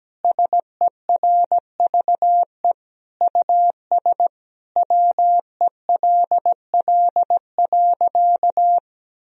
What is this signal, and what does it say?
Morse code, heard as a keyed tone, about 17 wpm, 700 Hz. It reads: SERVE US WELL.
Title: Sentences from Top 1000 Words - Review Part 4 - 17wpm